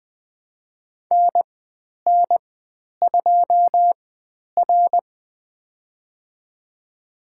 Morse code, heard as a keyed tone, about 20 wpm, spaced as for 10 wpm, 700 Hz. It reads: NN2R
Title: 102 Lesson 17g - Letter u - Call Signs 20wpm